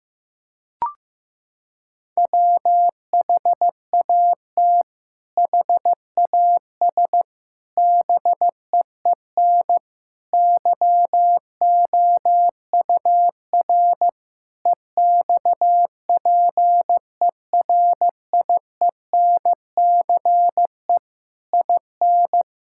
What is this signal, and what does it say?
Morse code, heard as a keyed tone, about 15 wpm, 700 Hz. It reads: WHAT HAS BEEN YOUR EXPERIENCE IN
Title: Sentences from Top 1000 Words - Review Part 6 - 15wpm